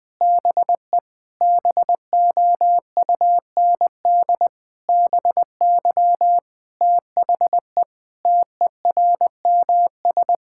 Morse code, heard as a keyed tone, about 21 wpm, 700 Hz. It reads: BE BOUND BY THE TERMS
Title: Sentences with 9-Word Phrases - 20wpm